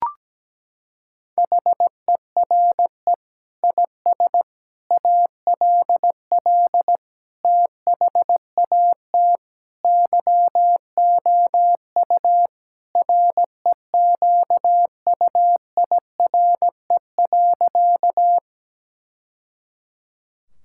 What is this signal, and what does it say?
Morse code, heard as a keyed tone, about 17 wpm, 700 Hz. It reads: HERE IS ALL THAT YOU REQUIRE.
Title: Sentences from Top 1000 Words - Review Part 6 - 17wpm